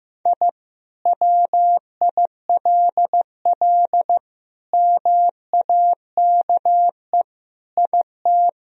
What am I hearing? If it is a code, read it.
Morse code, heard as a keyed tone, about 15 wpm, 700 Hz. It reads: I WILL MAKE IT
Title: Sentences from Top 1000 Words - Review Part 2 - 15wpm